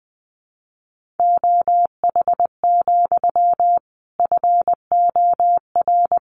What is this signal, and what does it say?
Morse code, heard as a keyed tone, about 20 wpm, 700 Hz. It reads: OH, FOR
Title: Sentences from Top 1400 Words 20wpm